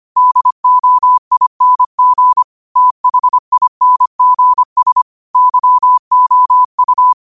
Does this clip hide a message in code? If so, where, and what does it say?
Yes: Morse code from 0.2 to 7.1 s, heard as a keyed tone, about 25 wpm, 1000 Hz. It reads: DOING THINGS YOU